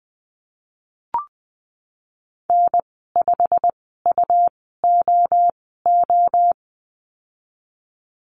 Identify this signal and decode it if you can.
Morse code, heard as a keyed tone, about 20 wpm, spaced as for 15 wpm, 700 Hz. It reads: N5UOO
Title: 167 Lesson 27g - Number 7 - Call Signs 20wpm